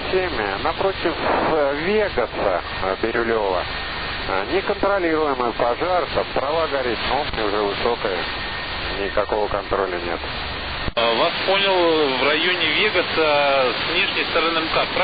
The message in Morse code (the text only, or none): none